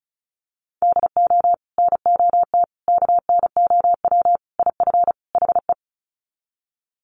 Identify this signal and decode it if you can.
Morse code, heard as a keyed tone, about 34 wpm, 700 Hz. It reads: DO NOT KNOW IF HE
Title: Sets of 5 Words Encore2 35wpm